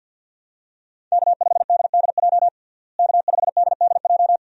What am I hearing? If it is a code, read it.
Morse code, heard as a keyed tone, about 49 wpm, 700 Hz. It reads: K5DDJ K5DDJ
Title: USA Call Signs - 5 Characters - Repeated 2x - 50wpm